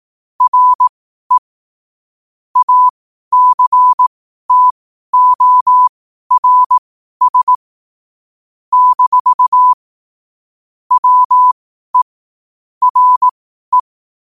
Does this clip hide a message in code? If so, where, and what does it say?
Yes: Morse code from 0.4 to 13.8 s, heard as a keyed tone, about 18 wpm, spaced as for 12 wpm, 1000 Hz. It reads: RE ACTORS - WE RE